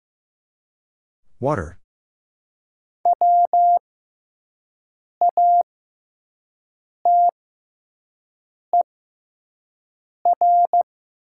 Morse code, heard as a keyed tone, about 15 wpm, spaced as for 5 wpm, 690 Hz. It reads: E WATER
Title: 6x Character Spacing - Top 100 Words - 15wpm